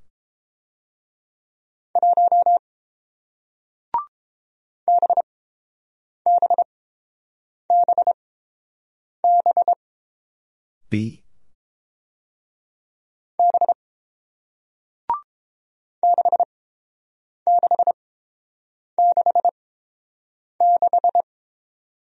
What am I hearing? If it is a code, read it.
Morse code, heard as a keyed tone, about 30 wpm, spaced as for 7 wpm, 700 Hz. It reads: 1 BBBB B 6666